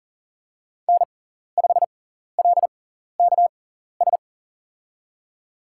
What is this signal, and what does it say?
Morse code, heard as a keyed tone, about 40 wpm, spaced as for 14 wpm, 700 Hz. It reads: N5LKS
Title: USA Call Signs - 6x Character Spacing - 40wpm